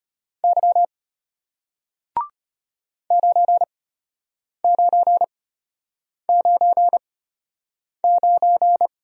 Morse code, heard as a keyed tone, about 33 wpm, spaced as for 8 wpm, 690 Hz. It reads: Y 9999